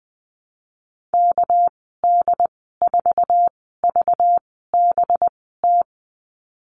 Morse code, heard as a keyed tone, about 20 wpm, spaced as for 14 wpm, 700 Hz. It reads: KD4VBT
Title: 245 Lesson 39g - Letter z - Call Signs 20wpm